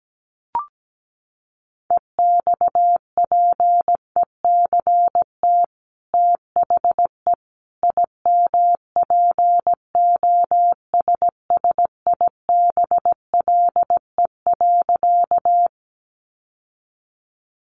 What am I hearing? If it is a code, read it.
Morse code, heard as a keyed tone, about 17 wpm, 700 Hz. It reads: EXPECT THE IMPOSSIBLE.
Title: Sentences from Top 1100 Words 17wpm